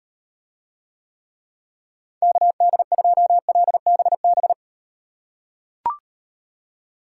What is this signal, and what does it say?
Morse code, heard as a keyed tone, about 34 wpm, 700 Hz. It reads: KD2LBB